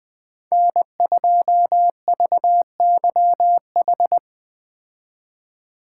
Morse code, heard as a keyed tone, about 20 wpm, 700 Hz. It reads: N2VYH
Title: Speed Racing - USA 1x3 Call Signs 17wpm